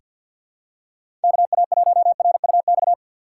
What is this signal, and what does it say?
Morse code, heard as a keyed tone, about 45 wpm, 700 Hz. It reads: KA1RUX
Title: USA Call Signs - 6 Characters - 4x Character Spacing - 50wpm